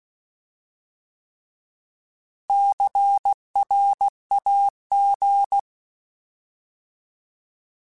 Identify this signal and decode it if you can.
Morse code, heard as a keyed tone, about 16 wpm, 800 Hz. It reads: CRAG
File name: Alphabetical / Alphabetical mp3s / Cs @15